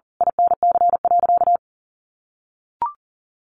Morse code, heard as a keyed tone, about 40 wpm, 700 Hz. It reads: INC.
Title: A Sound Of Thunder 40wpm